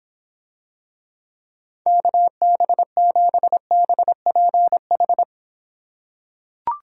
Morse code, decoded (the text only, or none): KB7BPH